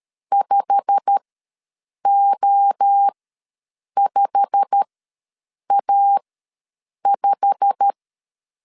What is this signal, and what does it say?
Morse code, heard as a keyed tone, about 13 wpm, spaced as for 7 wpm, 800 Hz. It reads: 5O5A5
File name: LEZIONI / 8 YJ45 / 8 YJ45 ws 12wpm 3146